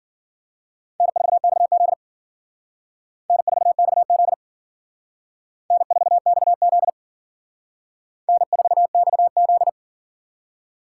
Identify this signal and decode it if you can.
Morse code, heard as a keyed tone, about 51 wpm, 700 Hz. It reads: N4XZ N4XZ N4XZ N4XZ